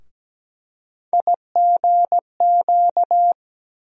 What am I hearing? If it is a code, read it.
Morse code, heard as a keyed tone, about 17 wpm, 700 Hz. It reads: IGQ
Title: USA Call Sign Postfixes - 3 Letters 17wpm